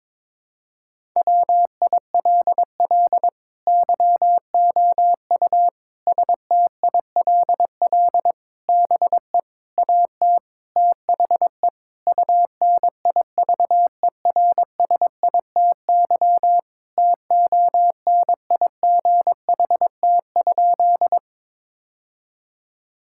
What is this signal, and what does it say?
Morse code, heard as a keyed tone, about 21 wpm, 700 Hz. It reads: WILL YOU STILL BE AT THE UNIVERSITY TONIGHT?
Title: Sentences from Top 1000 Words - Review Part 5 - 22wpm